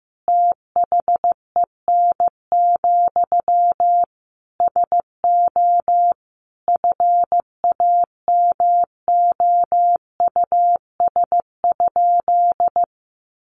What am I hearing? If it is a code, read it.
Morse code, heard as a keyed tone, about 15 wpm, 700 Hz. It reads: THEN, SO FAMOUS?
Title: Sentences from Top 1500 Words 15wpm